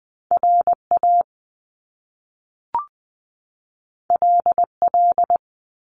Morse code, heard as a keyed tone, about 20 wpm, 700 Hz. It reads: RA LL
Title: Top Two Letter Combinations in Words 20wpm